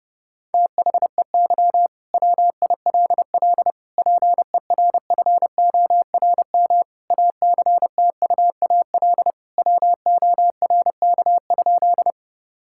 Morse code, heard as a keyed tone, about 30 wpm, 700 Hz. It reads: THEY WILL PERFORM ACTUAL WORK?
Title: Sentences from Top 1000 Words - Review Part 5 - 30wpm